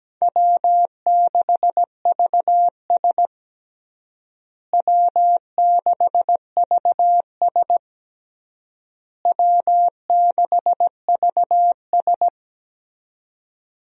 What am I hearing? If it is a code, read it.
Morse code, heard as a keyed tone, about 17 wpm, 700 Hz. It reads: W6VS W6VS W6VS
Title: USA Call Signs - 4 Characters - Repeated 3x - 17wpm